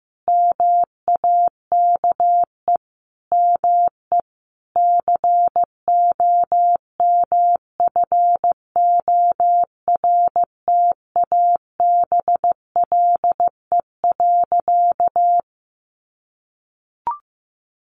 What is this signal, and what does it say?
Morse code, heard as a keyed tone, about 15 wpm, 700 Hz. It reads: MAKE ME COMFORTABLE.
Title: Sentences from Top 1500 Words 15wpm